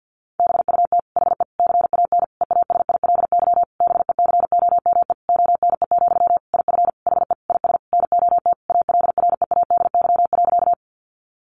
Morse code, heard as a keyed tone, about 50 wpm, 700 Hz. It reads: BUT HE CAN EASILY BECOME ONE, IF HE IS NOT ALREADY.
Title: Sentences from Top 900 Words 50wpm